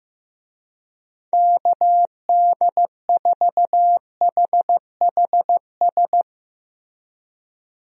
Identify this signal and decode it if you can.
Morse code, heard as a keyed tone, about 16 wpm, 700 Hz. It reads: KD4HHS
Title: USA Call Signs - 6 Characters - Repeated 2x - 15wpm